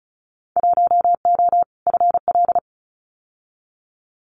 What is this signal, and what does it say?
Morse code, heard as a keyed tone, about 34 wpm, 700 Hz. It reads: 1O FL
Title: Winter Field Day Exchanges 35wpm